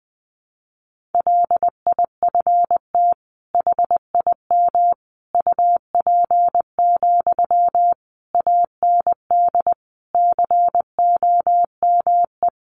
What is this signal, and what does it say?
Morse code, heard as a keyed tone, about 20 wpm, 700 Hz. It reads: LIFT HIM UP, AND COME
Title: Sentences from Top 1000 Words 20wpm